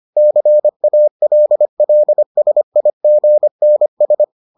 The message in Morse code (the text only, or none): CALLSIGNS